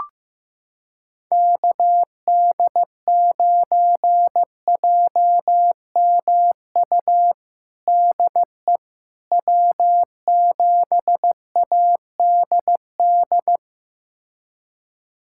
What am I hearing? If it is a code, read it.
Morse code, heard as a keyed tone, about 15 wpm, 700 Hz. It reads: KD9JMU DE W7ADD
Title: USA Callsign DE Callsign 15wpm